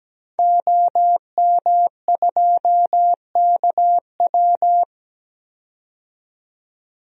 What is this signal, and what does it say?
Morse code, heard as a keyed tone, about 17 wpm, 700 Hz. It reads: OM2KW
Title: DXCC-100 Call Signs - 5 Characters - 17wpm